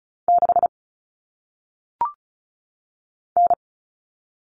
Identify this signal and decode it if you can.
Morse code, heard as a keyed tone, about 36 wpm, 700 Hz. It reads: 6 N
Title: Common Cut Numbers - 1-Digit 35wpm